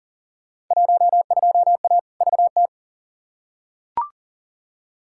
Morse code, heard as a keyed tone, about 38 wpm, 700 Hz. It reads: 12A VT
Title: Field Day Exchanges Encore - 40wpm